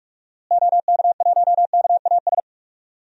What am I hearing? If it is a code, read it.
Morse code, heard as a keyed tone, about 44 wpm, 700 Hz. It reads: OK1KAS
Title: DXCC-100 Call Signs - 6 Characters - Repeated 3x - 45wpm